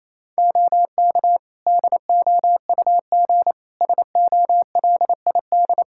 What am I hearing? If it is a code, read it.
Morse code, heard as a keyed tone, about 28 wpm, 700 Hz. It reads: OK DOUG SOLID
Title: K1USN SST QSO Practice 28wpm